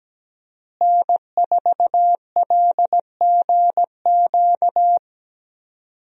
Morse code, heard as a keyed tone, about 17 wpm, 700 Hz. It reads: N4LGQ
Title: USA Call Signs - 5 Characters - Repeated 2x - 17wpm